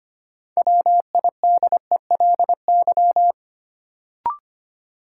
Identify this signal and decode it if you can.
Morse code, heard as a keyed tone, about 25 wpm, 700 Hz. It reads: WIDELY
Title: Sentences from Top 1200 Words - New Word Drill - 25wpm